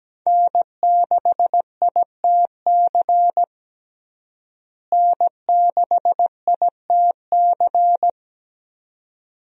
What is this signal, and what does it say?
Morse code, heard as a keyed tone, about 17 wpm, 700 Hz. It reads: N6ITC N6ITC
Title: USA Call Signs - 5 Characters - Repeated 2x - 17wpm